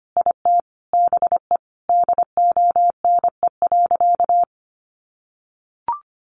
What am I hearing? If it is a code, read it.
Morse code, heard as a keyed tone, about 25 wpm, 700 Hz. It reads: IT BE DONE.
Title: Sentences from Top 500 Words 25wpm